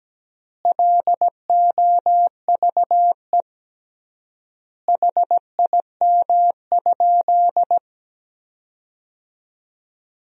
Morse code, heard as a keyed tone, about 17 wpm, 700 Hz. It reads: LOVE HIM?
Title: 3x Word-Spacing - Sentences from Top 400 Words - 17wpm